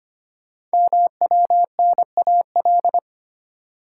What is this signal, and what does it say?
Morse code, heard as a keyed tone, about 25 wpm, 700 Hz. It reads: MWNAL
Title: Code Groups - 5 Letters - 25wpm